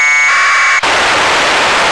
Packet radio (AX.25, as in APRS).
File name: lu4aao / globo19 1 LU2DPW